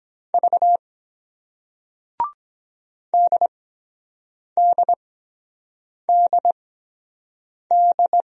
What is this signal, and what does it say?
Morse code, heard as a keyed tone, about 23 wpm, spaced as for 7 wpm, 700 Hz. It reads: V DDDD